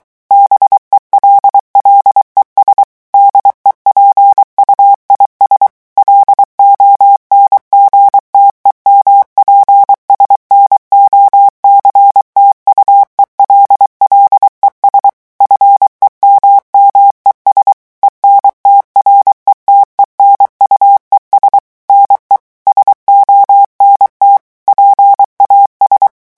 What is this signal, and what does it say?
Morse code, heard as a keyed tone, about 23 wpm, 800 Hz. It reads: BELLES DEPUIS LONGTEMPS NOCTUEL LES FEMMES ENTRETENUES NE SONT PAS
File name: morse mp3 / revision9 citations humoristiques vitesse 22 mots minute